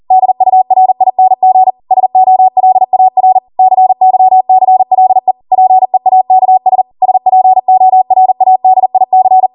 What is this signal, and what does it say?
Morse code, heard as a keyed tone, about 40 wpm, 750 Hz. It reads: DURING SOLAR CYCLE PEAKS SPORADIC